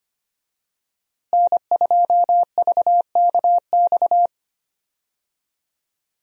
Morse code, heard as a keyed tone, about 25 wpm, 700 Hz. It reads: N2VKX
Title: USA Call Signs - 5 Characters - Repeated 2x - 25wpm